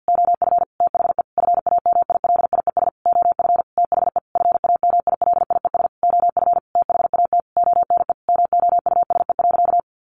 Morse code, heard as a keyed tone, about 50 wpm, 700 Hz. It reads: OF THE FAMILIES OF THE FAMILIES OF THAT ONE MOUSE.
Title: A Sound Of Thunder 50wpm